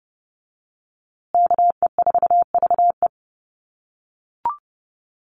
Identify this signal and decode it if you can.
Morse code, heard as a keyed tone, about 28 wpm, 700 Hz. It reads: KE4VE
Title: USA Call Signs Repeated 3x 30wpm